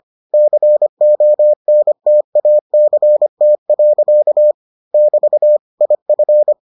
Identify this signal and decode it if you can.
Morse code, heard as a keyed tone, about 25 wpm, 600 Hz. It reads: CONTACT. <BT> IF